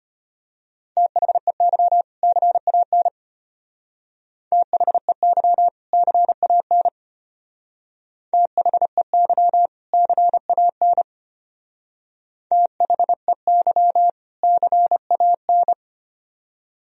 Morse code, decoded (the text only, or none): THEY CAN THEY CAN THEY CAN THEY CAN